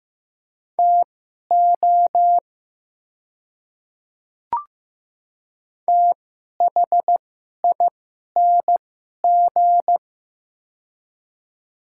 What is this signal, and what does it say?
Morse code, heard as a keyed tone, about 15 wpm, spaced as for 11 wpm, 700 Hz. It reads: TO THING
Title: 2x Character Spacing - Top 100 Words - 15wpm